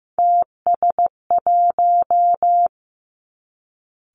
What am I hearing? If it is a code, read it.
Morse code, heard as a keyed tone, about 15 wpm, 700 Hz. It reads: TS1
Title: Three Challenging Letter-Number Combinations 15wpm